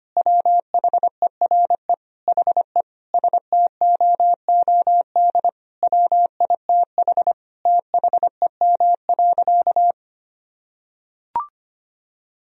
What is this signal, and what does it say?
Morse code, heard as a keyed tone, about 25 wpm, 700 Hz. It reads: WHERE HE STOOD WITH THEM.
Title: Sentences from Top 1000 Words - Review Part 2 - 25wpm